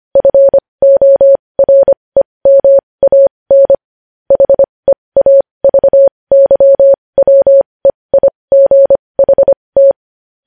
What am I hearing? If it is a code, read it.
Morse code, heard as a keyed tone, about 25 wpm, spaced as for 20 wpm, 550 Hz. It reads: FOREMAN HEAVYWEIGHT
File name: podcast / News headlines-1x-25x20